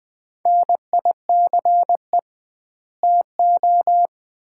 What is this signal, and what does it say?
Morse code, heard as a keyed tone, about 21 wpm, 700 Hz. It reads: NICE TO
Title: Common QSO Phrases - 2x Word-Spacing - 20wpm